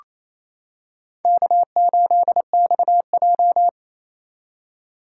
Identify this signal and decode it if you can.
Morse code, heard as a keyed tone, about 28 wpm, 700 Hz. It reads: K8XJ
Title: K1USN SST QSO Practice - 28wpm